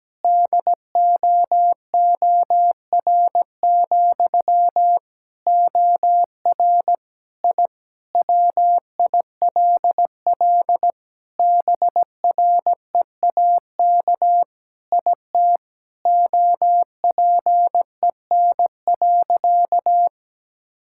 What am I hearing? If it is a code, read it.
Morse code, heard as a keyed tone, about 17 wpm, 700 Hz. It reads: DOOR, OR I WILL BREAK IT OPEN.
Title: Sentences from Top 1000 Words - Review Part 6 - 17wpm